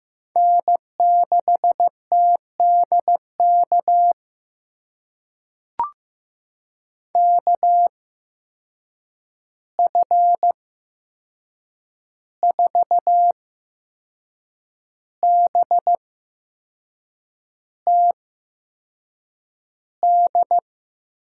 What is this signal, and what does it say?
Morse code, heard as a keyed tone, about 15 wpm, spaced as for 6 wpm, 700 Hz. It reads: N6TDK KF4BTD
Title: USA Call Signs - 8x Character Spacing - 15wpm